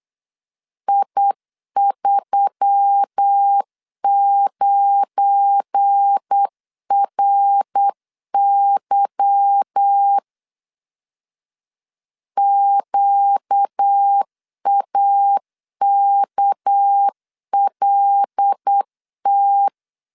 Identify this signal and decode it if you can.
Morse code, heard as a keyed tone, about 8 wpm, 800 Hz. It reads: I39RY QAKLT
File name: LEZIONI / 8 YJ45 / 8 Y 8wpm 3245